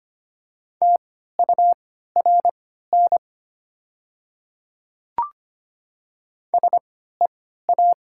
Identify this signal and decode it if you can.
Morse code, heard as a keyed tone, about 25 wpm, spaced as for 14 wpm, 700 Hz. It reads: TURN SEA
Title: 3x Character Spacing - Top 300 Words - 25wpm